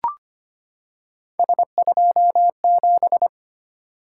Morse code, heard as a keyed tone, about 25 wpm, 700 Hz. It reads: S27
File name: Three Letter-Number Combinations - 25wpm